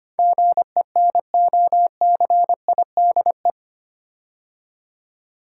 Morse code, heard as a keyed tone, about 25 wpm, 700 Hz. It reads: GENOCIDE
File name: 8 Letter Words - First 3 Letters Given - 25wpm